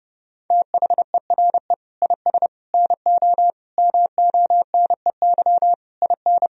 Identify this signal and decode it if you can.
Morse code, heard as a keyed tone, about 30 wpm, 700 Hz. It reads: THERE IS NO MONEY IN
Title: Sentences from Top 1000 Words - Review Part 3 - 30wpm